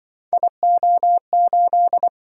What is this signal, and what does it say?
Morse code, heard as a keyed tone, about 25 wpm, 700 Hz. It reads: IO8
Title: Speed-Racing - Three Letter-Number Combinations - 20wpm